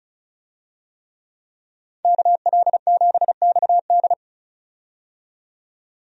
Morse code, heard as a keyed tone, about 36 wpm, 700 Hz. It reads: KL7XD
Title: DXCC-100 Call Signs - 5 Characters - 35wpm